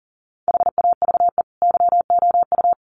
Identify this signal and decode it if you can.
Morse code, heard as a keyed tone, about 40 wpm, 700 Hz. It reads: HAVE YOU
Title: Sentences from Top 1200 Words 40wpm